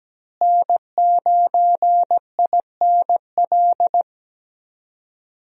Morse code, heard as a keyed tone, about 17 wpm, 700 Hz. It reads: N9INL
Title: USA Call Signs - 5 Characters - Repeated 3x - 17wpm